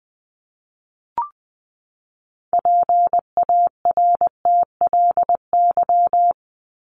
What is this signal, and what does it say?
Morse code, heard as a keyed tone, about 20 wpm, 700 Hz. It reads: PARTLY
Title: LY Suffix - Top 100 Words with Base Words 20wpm